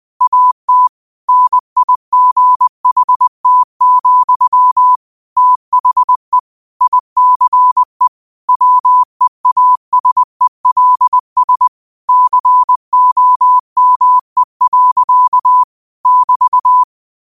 Morse code, heard as a keyed tone, about 20 wpm, 1000 Hz. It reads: AT NIGHT, THE ICE WEASELS COME. <BT>
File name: qotdfiles / 20wpm